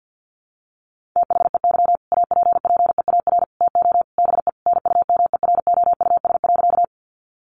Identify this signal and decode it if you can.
Morse code, heard as a keyed tone, about 50 wpm, 700 Hz. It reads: THEY APPEAR TO BE NUMEROUS.
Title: Sentences from Top 1400 Words 50wpm